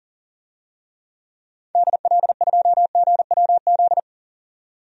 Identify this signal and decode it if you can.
Morse code, heard as a keyed tone, about 37 wpm, 700 Hz. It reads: DL2GWZ
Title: DXCC-100 Call Signs - 6 Characters - 40wpm